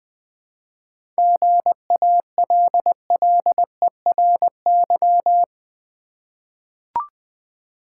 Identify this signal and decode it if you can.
Morse code, heard as a keyed tone, about 20 wpm, 700 Hz. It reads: GALLERY
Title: 7 Letter Words - First 3 Letters Given - 20wpm